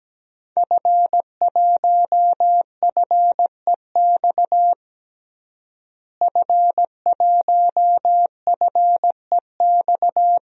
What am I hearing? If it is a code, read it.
Morse code, heard as a keyed tone, about 17 wpm, 700 Hz. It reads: F1FEX F1FEX
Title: DXCC-100 Call Signs - 5 Characters - Repeated 2x - 17wpm